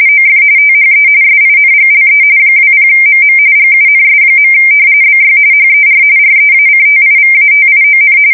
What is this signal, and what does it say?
RTTY, about 45.45 baud. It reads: OF THE SINGLE 20MM GUNS. BY HER REACTIVATION
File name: fsk170 / battleshipnj